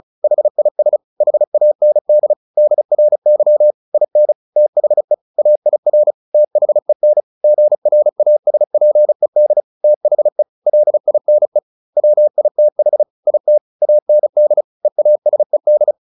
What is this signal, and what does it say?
Morse code, heard as a keyed tone, about 36 wpm, 600 Hz. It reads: HIS HAND DRY IN THE AIR THEN GRASPED THE LINE WITH IT AND EASED
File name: the old man and the sea 2 35wpm